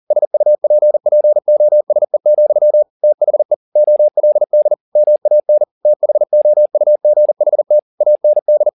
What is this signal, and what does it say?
Morse code, heard as a keyed tone, about 40 wpm, 600 Hz. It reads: SUPPOSE, THE OLD MAN THOUGHT AND